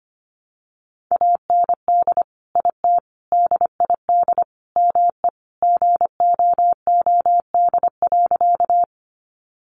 Morse code, heard as a keyed tone, about 25 wpm, 700 Hz. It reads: AND IT DID ME GOOD.
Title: Sentences from Top 200 Words 25wpm